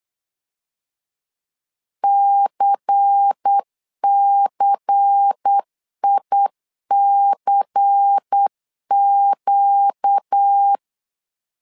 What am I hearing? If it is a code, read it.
Morse code, heard as a keyed tone, about 8 wpm, 800 Hz. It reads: CCICQ
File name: LEZIONI / 5 LBVC / 5 LBVC 8wpm 2935